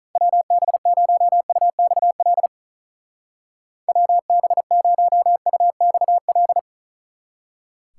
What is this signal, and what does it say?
Morse code, heard as a keyed tone, about 45 wpm, 700 Hz. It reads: WB0UXL WB0UXL